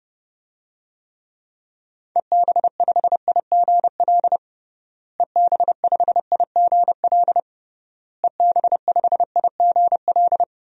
Morse code, heard as a keyed tone, about 32 wpm, 700 Hz. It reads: EB5IGL EB5IGL EB5IGL
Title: DXCC-100 Call Signs - 6 Characters - Repeated 3x - 30wpm